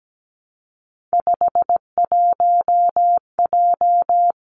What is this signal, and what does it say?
Morse code, heard as a keyed tone, about 17 wpm, 700 Hz. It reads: 51J
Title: Three Challenging Letter-Number Combinations 17wpm